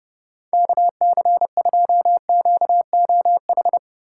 Morse code, heard as a keyed tone, about 28 wpm, 700 Hz. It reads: KC2QOH